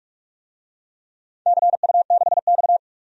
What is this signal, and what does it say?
Morse code, heard as a keyed tone, about 44 wpm, 700 Hz. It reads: CU6X
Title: DXCC-100 Call Signs - 4 Characters - 45wpm